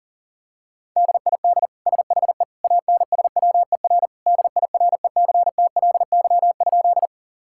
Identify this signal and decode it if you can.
Morse code, heard as a keyed tone, about 40 wpm, 700 Hz. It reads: DID SHE ANSWER DIRECTLY?
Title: Sentences from Top 1000 Words - Review Part 5 - 40wpm